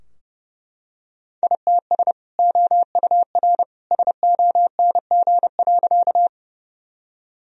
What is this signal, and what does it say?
Morse code, heard as a keyed tone, about 32 wpm, 700 Hz. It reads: ITS OUR SONG.
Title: Sentences from Top 1000 Words - Review Part 2 - 30wpm